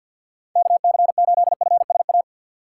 Morse code, heard as a keyed tone, about 51 wpm, 700 Hz. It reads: KC8FSA
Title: USA Call Signs - 4x Character Spacing - 50wpm